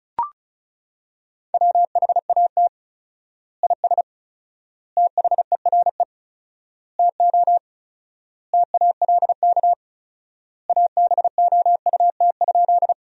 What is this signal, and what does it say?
Morse code, heard as a keyed tone, about 34 wpm, 700 Hz. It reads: WHAT IS THERE TO TALK ABOUT?
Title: 4x Word-Spacing - Sentences from Top 400 Words - 35wpm